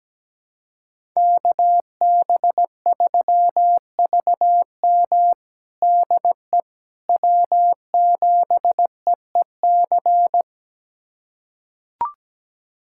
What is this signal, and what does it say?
Morse code, heard as a keyed tone, about 17 wpm, 700 Hz. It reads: KB3VM DE W7EEC
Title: USA Callsign DE Callsign - 5 Characters - 17wpm